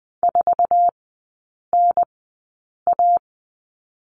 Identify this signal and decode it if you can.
Morse code, heard as a keyed tone, about 20 wpm, spaced as for 8 wpm, 700 Hz. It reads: 4NA
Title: 074 Lesson 13e - Letter L - Three Characters 20wpm